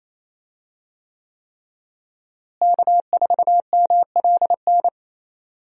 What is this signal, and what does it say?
Morse code, heard as a keyed tone, about 28 wpm, 700 Hz. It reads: K4MLN
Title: USA Call Signs - 5 Characters - Repeated 3x - 28wpm